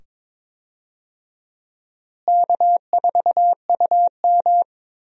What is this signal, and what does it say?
Morse code, heard as a keyed tone, about 22 wpm, 700 Hz. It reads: K4UM
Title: USA Call Signs - 4 Characters - Repeated 2x - 22wpm